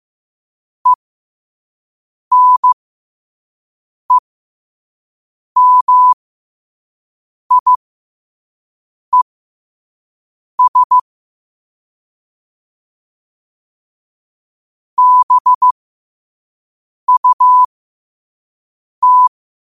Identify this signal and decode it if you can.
Morse code, heard as a keyed tone, about 15 wpm, spaced as for 5 wpm, 1000 Hz. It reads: ENEMIES BUT
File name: qotdfiles / 5wpm